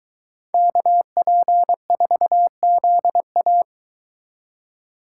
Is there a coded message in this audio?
Yes: Morse code from 0.5 to 3.6 s, heard as a keyed tone, about 22 wpm, 700 Hz. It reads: KP4ZA